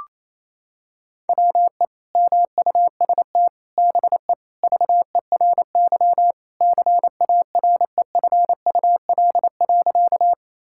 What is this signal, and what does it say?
Morse code, heard as a keyed tone, about 28 wpm, 700 Hz. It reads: WE MUST BE VERY CAREFUL.
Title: Sentences from Top 1600 Words 28wpm